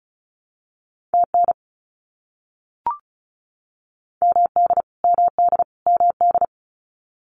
Morse code, heard as a keyed tone, about 34 wpm, 700 Hz. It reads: TN MD MD MD
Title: US States Repeated 3x 35wpm